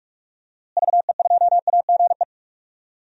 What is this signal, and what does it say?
Morse code, heard as a keyed tone, about 45 wpm, 700 Hz. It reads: VE2AGE
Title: DXCC-100 Call Signs - 6 Characters - Repeated 3x - 45wpm